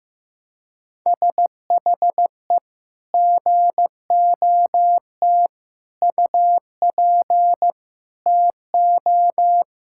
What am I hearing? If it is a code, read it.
Morse code, heard as a keyed tone, about 15 wpm, 700 Hz. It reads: SHE GOT UP TO
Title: Sentences from Top 1600 Words 15wpm